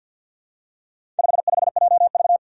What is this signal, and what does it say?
Morse code, heard as a keyed tone, about 53 wpm, 700 Hz. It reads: 55JV
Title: Four Challenging Letter-Number Combinations 50wpm